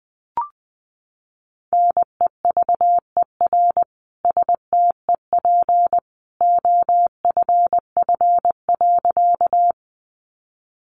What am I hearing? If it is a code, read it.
Morse code, heard as a keyed tone, about 19 wpm, 700 Hz. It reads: NEVER STEP OFF.
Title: A Sound Of Thunder 20wpm